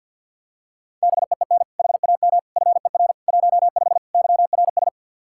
Morse code, heard as a keyed tone, about 49 wpm, 700 Hz. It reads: BEEN HAM FER 15 YRS
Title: Practice QSO - Exchange 3 - More Info - 50wpm